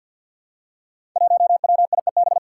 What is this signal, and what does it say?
Morse code, heard as a keyed tone, about 48 wpm, 700 Hz. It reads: 1WIEB
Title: Five Challenging Letter-Number Combinations 50wpm